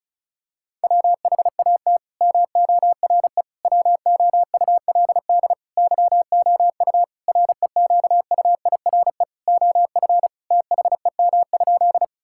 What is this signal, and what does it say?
Morse code, heard as a keyed tone, about 34 wpm, 700 Hz. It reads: WHAT MORE WOULD YOU REQUIRE OF THEM?
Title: Sentences from Top 1000 Words - Review Part 6 - 35wpm